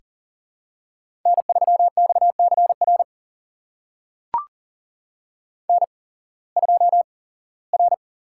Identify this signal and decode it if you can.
Morse code, heard as a keyed tone, about 40 wpm, spaced as for 21 wpm, 700 Hz. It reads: N3XCR N2R
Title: USA Call Signs - 5 Characters - 8x Character Spacing - 40wpm